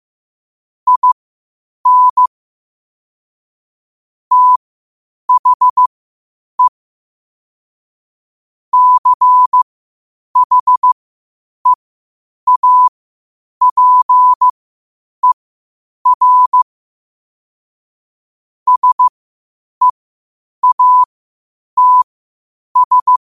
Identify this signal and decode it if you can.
Morse code, heard as a keyed tone, about 15 wpm, spaced as for 8 wpm, 1000 Hz. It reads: IN THE CHEAPER SEATS
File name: qotdfiles / 8wpm